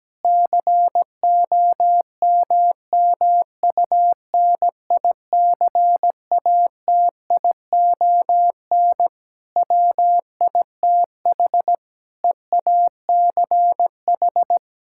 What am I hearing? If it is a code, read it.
Morse code, heard as a keyed tone, about 17 wpm, 700 Hz. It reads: COMMUNICATION WITH EACH
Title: Sentences from Top 1000 Words - Review Part 6 - 17wpm